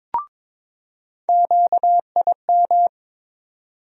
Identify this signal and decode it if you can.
Morse code, heard as a keyed tone, about 22 wpm, 700 Hz. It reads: QIM
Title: USA Call Sign Postfixes - 3 Letters 22wpm